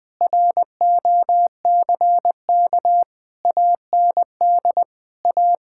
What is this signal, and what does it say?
Morse code, heard as a keyed tone, about 21 wpm, 700 Hz. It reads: ROCK AND A
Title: Sets of 8 Words - 20wpm